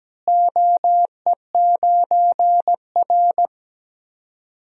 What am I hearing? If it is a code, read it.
Morse code, heard as a keyed tone, about 17 wpm, 700 Hz. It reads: OE9R
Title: DXCC-100 Call Signs - 4 Characters - Repeated 3x - 17wpm